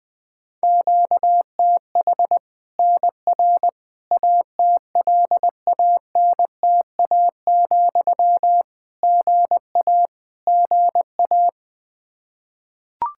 Morse code, heard as a keyed tone, about 20 wpm, 700 Hz. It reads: QTH NR ATLANTA, GA GA
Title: Practice QSO - Exchange 1 - Beginning - 20wpm